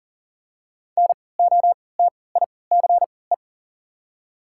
Morse code, heard as a keyed tone, about 40 wpm, spaced as for 23 wpm, 700 Hz. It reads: NOTICE
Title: 3x Character Spacing - Top 500 Words - 40wpm